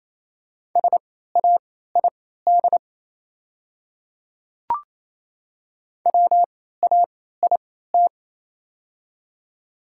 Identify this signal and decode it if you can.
Morse code, heard as a keyed tone, about 28 wpm, spaced as for 16 wpm, 690 Hz. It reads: SAID WAIT
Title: 3x Character Spacing - Top 500 Words - 28wpm